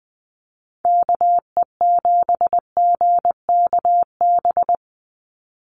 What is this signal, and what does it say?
Morse code, heard as a keyed tone, about 20 wpm, 700 Hz. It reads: KE7GKB
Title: USA Call Signs 20wpm